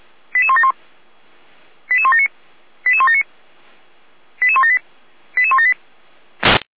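Five-tone selective call (ZVEI), heard as quick runs of five tones, tones of about 70 ms. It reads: 88171 88179 88179 88178 88178
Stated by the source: ZVEI five-tone selective call 88402 received on an SDR.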